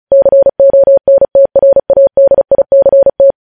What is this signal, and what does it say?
Morse code, heard as a keyed tone, about 36 wpm, 550 Hz. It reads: CONTRADICT
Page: news headlines in morse code at 35 wpm